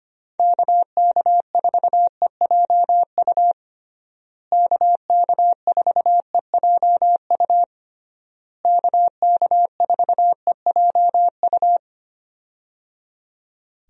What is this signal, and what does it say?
Morse code, heard as a keyed tone, about 25 wpm, 700 Hz. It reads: KK4EJU KK4EJU KK4EJU
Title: USA Call Signs - 6 Characters - Repeated 3x - 25wpm